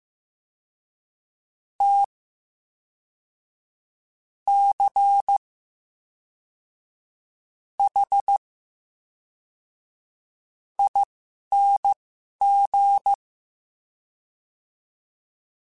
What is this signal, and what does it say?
Morse code, heard as a keyed tone, about 15 wpm, spaced as for 11 wpm, 800 Hz. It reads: T C H ING
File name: Special INGs / Special INGs-mp3s / VerySlowToFaster Ing @10